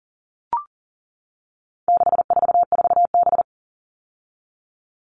Morse code, heard as a keyed tone, about 37 wpm, 700 Hz. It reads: 644B
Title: All Cut Numbers - 4-Digit 40wpm